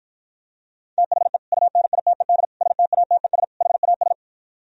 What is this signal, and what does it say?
Morse code, heard as a keyed tone, about 52 wpm, 700 Hz. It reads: THE UNITED STATES HAS